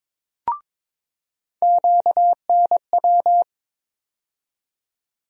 Morse code, heard as a keyed tone, about 22 wpm, 700 Hz. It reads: QNW
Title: USA Call Sign Postfixes - 3 Letters 22wpm